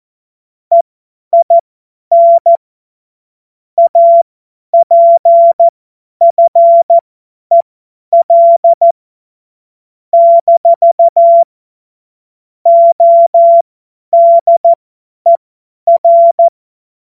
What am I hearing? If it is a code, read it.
Morse code, heard as a keyed tone, about 14 wpm, spaced as for 10 wpm, 680 Hz. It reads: EIN APFEL - ODER
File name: Morsekurs / Witze / Witz 06. April 2021, 50 F70